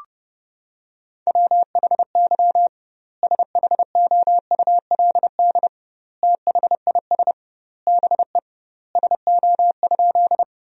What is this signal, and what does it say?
Morse code, heard as a keyed tone, about 32 wpm, 700 Hz. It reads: WHY SHOULD THIS BE SO?
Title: 2x Word-Spacing - Sentences from Top 300 Words - 30wpm